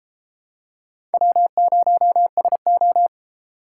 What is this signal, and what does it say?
Morse code, heard as a keyed tone, about 32 wpm, 700 Hz. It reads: W0SO